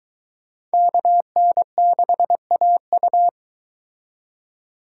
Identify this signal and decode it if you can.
Morse code, heard as a keyed tone, about 22 wpm, 700 Hz. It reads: KN6AU